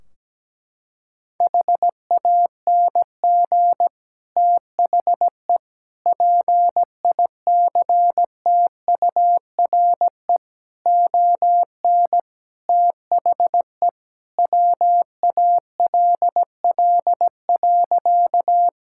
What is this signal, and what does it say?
Morse code, heard as a keyed tone, about 17 wpm, 700 Hz. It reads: HANG THE PICTURE ON THE WALL.
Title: Sentences from Top 1600 Words 17wpm